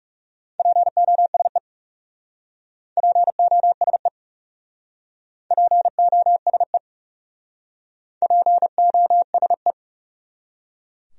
Morse code, heard as a keyed tone, about 38 wpm, 700 Hz. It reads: POSE POSE POSE POSE